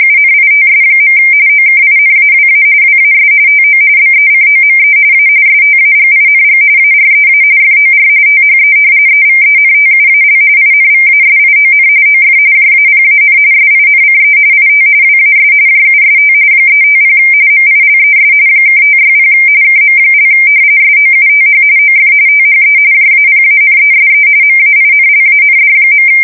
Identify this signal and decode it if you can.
RTTY, about 45.45 baud. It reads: UNITS. OKINAWA WAS TO BECOME A STAGING AREA FOR THE EVENTUAL INVASION OF JAPAN SINCE IT WAS JUST 350 MILES (560 KM) SOUTH OF THE JAPANESE MAINLAND.
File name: fsk170 / usnavy4